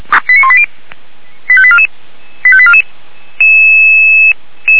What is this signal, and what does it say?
Five-tone selective call (ZVEI), heard as quick runs of five tones, tones of about 70 ms. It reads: 89180 85844 85844
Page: /464/SDR/ZVEI/89135